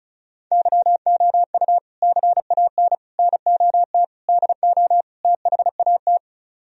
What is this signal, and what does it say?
Morse code, heard as a keyed tone, about 37 wpm, 700 Hz. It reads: YOU CAN NOT DO THAT
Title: Speed-Racing - Sets of 5 Words 30wpm